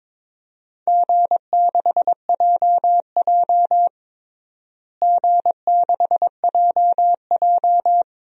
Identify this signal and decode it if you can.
Morse code, heard as a keyed tone, about 23 wpm, 700 Hz. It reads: G6JJ G6JJ
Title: DXCC-100 Call Signs - 4 Characters - Repeated 2x - 22wpm